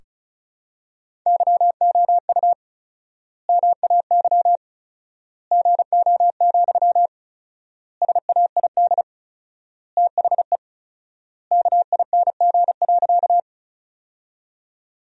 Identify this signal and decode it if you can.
Morse code, heard as a keyed tone, about 36 wpm, 700 Hz. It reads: YOU MAY GO, SAID THE KING.
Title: 4x Word-Spacing - Sentences from Top 400 Words - 35wpm